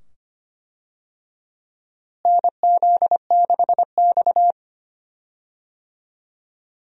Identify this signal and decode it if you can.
Morse code, heard as a keyed tone, about 25 wpm, 700 Hz. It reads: NZ6X
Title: USA Call Signs - 4 Characters - Repeated 2x - 25wpm